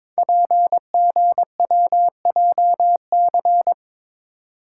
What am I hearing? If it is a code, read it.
Morse code, heard as a keyed tone, about 22 wpm, 700 Hz. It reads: PGWJC
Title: Code Groups - 5 Letters - 22wpm